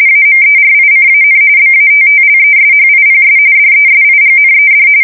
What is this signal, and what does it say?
RTTY, about 45.45 baud. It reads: AMERICAN BATTLESHIPS DESIGNED